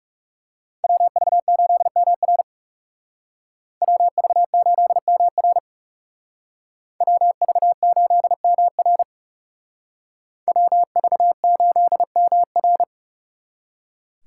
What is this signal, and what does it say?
Morse code, heard as a keyed tone, about 35 wpm, 700 Hz. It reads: WV8MR WV8MR WV8MR WV8MR
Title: Speed Racing - USA 2x2 Call Signs 30wpm